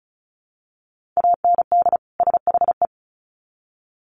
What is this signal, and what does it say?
Morse code, heard as a keyed tone, about 34 wpm, 700 Hz. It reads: AND SHE
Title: Sets of 2 Words 35wpm